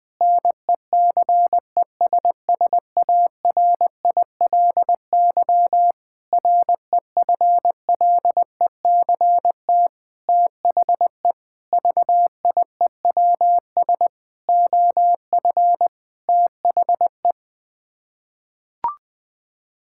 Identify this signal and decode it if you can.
Morse code, heard as a keyed tone, about 20 wpm, 700 Hz. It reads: NECESSARILY REFLECT THE VIEWS OF THE
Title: Sets of 8 Words - 20wpm